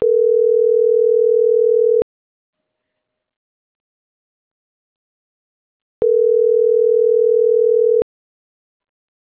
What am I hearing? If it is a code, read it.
Ringback tone (North America), 440 + 480 Hz.